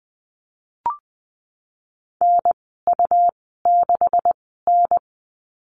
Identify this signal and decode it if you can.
Morse code, heard as a keyed tone, about 20 wpm, spaced as for 15 wpm, 700 Hz. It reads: NU6N
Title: 231 Lesson 37g - Letter x - Call Signs 20wpm